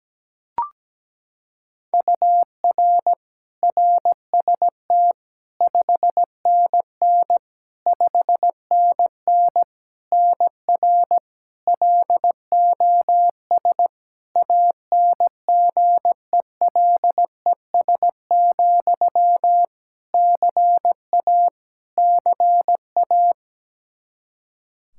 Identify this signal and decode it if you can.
Morse code, heard as a keyed tone, about 17 wpm, 700 Hz. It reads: UR RST 5NN 5NN NR LOS ANGELES, CA CA
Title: K1USN SST QSO Practice 17wpm